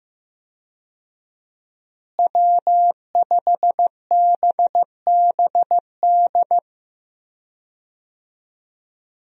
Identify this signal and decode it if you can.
Morse code, heard as a keyed tone, about 15 wpm, 700 Hz. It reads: W5BBD
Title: USA Call Signs - 5 Characters - Repeated 2x - 15wpm